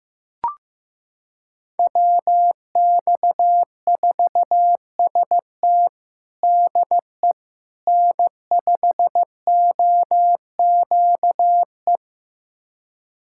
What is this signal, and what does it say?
Morse code, heard as a keyed tone, about 16 wpm, 700 Hz. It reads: WX4ST DE N5OQE
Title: USA Callsign DE Callsign 15wpm